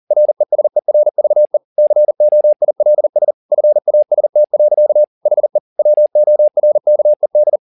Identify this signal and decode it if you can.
Morse code, heard as a keyed tone, about 40 wpm, 600 Hz. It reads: RESERVE COILS FAST. HE WORKED